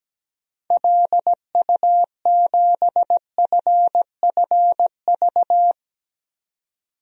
Morse code, heard as a keyed tone, about 17 wpm, 700 Hz. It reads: LU7FFV
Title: DXCC-100 Call Signs - 6 Characters - 17wpm